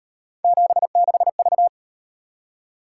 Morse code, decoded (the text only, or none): 76V